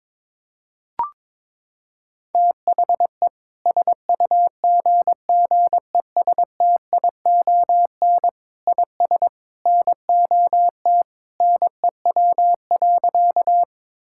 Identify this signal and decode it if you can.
Morse code, heard as a keyed tone, about 22 wpm, 700 Hz. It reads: THE SUGGESTION IS NOT NEW.
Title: Sentences from Top 1600 Words 22wpm